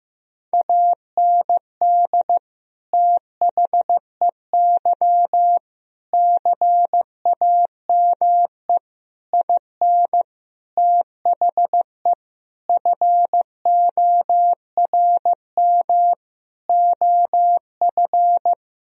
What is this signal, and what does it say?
Morse code, heard as a keyed tone, about 15 wpm, 700 Hz. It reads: AND THEY CAME IN THE FORM OF
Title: Sentences from Top 1000 Words - Review Part 1 - 15wpm